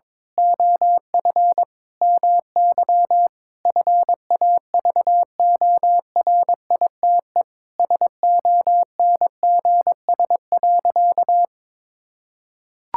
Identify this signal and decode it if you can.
Morse code, heard as a keyed tone, about 21 wpm, 700 Hz. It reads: OF MY FAVORITE SONGS.
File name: practice / 22wpm / Sentences with 4-Word Phrases - 22wpm